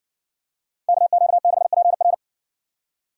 Morse code, heard as a keyed tone, about 60 wpm, 700 Hz. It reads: KQ6PR